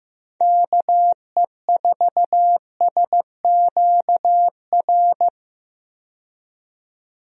Morse code, heard as a keyed tone, about 15 wpm, 700 Hz. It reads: KE4SQR
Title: USA Call Signs - 6 Characters - 15wpm